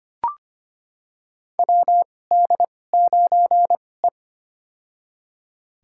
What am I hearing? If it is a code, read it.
Morse code, heard as a keyed tone, about 25 wpm, spaced as for 18 wpm, 700 Hz. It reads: WD9E
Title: USA Call Signs - 2x Character Spacing - 25wpm